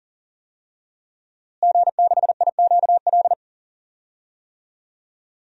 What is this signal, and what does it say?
Morse code, heard as a keyed tone, about 40 wpm, 700 Hz. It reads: G6IQL
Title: DXCC-100 Call Signs - 5 Characters - Repeated 2x - 40wpm